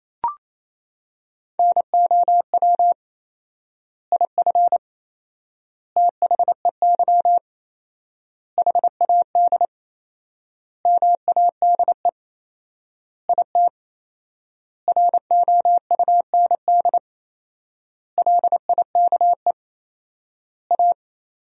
Morse code, heard as a keyed tone, about 28 wpm, 700 Hz. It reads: NOW IF THEY HAD MADE IT ROUND LIKE A
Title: 4x Word-Spacing - Sentences from Top 500 Words - 28wpm